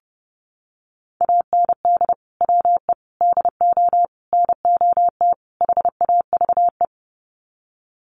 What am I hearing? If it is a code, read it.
Morse code, heard as a keyed tone, about 30 wpm, 700 Hz. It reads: AND WE DO NOT HAVE
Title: Sets of 5 Words Encore 30wpm